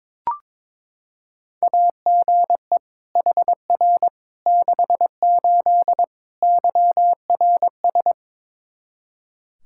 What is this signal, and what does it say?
Morse code, heard as a keyed tone, about 22 wpm, 700 Hz. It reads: AGE HR 68 YRS
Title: Practice QSO - Exchange 3 - More Info - 22wpm